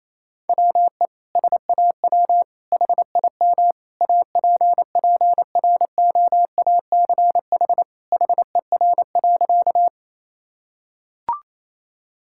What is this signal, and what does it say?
Morse code, heard as a keyed tone, about 28 wpm, 700 Hz. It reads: WE SAW HIM APPROACH HER.
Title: Sentences from Top 1000 Words - Review Part 6 - 28wpm